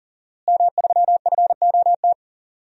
Morse code, heard as a keyed tone, about 40 wpm, 700 Hz. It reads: M3FOT
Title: DXCC-100 Call Signs - 5 Characters - Repeated 3x - 40wpm